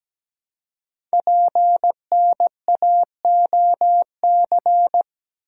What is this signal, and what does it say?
Morse code, heard as a keyed tone, about 17 wpm, 700 Hz. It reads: PNAOC
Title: Code Groups - 5 Letters - 17wpm